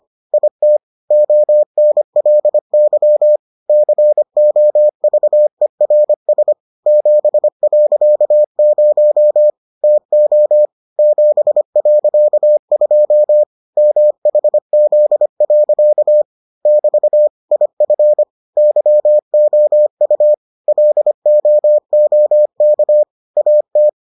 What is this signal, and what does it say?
Morse code, heard as a keyed tone, about 25 wpm, 600 Hz. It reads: IT ONLY COVERS 7.0 TO 7.2 MHZ. <BT> IF YOU LOOK AT